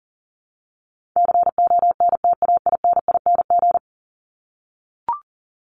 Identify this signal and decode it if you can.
Morse code, heard as a keyed tone, about 40 wpm, 700 Hz. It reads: CONTAINING
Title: ING Suffix - Top 100 Words with Base Words 40wpm